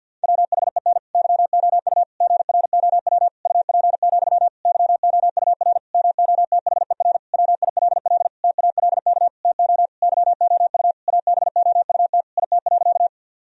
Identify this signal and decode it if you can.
Morse code, heard as a keyed tone, about 49 wpm, 700 Hz. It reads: WHEN YOU GROW UP, YOUR MOTHER WILL TALK TO YOU ABOUT IT.
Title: Sentences from Top 1000 Words - Review Part 1 - 50wpm